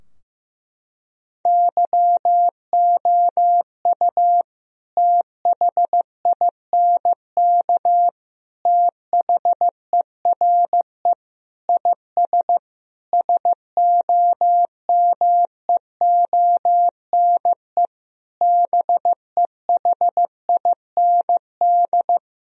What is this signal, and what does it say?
Morse code, heard as a keyed tone, about 15 wpm, 700 Hz. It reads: YOU THINK THERE IS SOMEONE BEHIND
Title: Sentences from Top 1000 Words - Review Part 4 - 15wpm